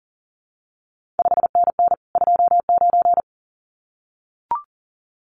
Morse code, heard as a keyed tone, about 42 wpm, 700 Hz. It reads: 5NN 29
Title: CQ WW DX Contest Exchange Encore - 40wpm